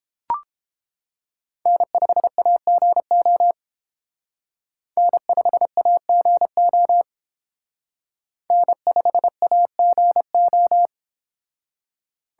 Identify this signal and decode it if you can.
Morse code, heard as a keyed tone, about 30 wpm, 700 Hz. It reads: N5AGO N5AGO N5AGO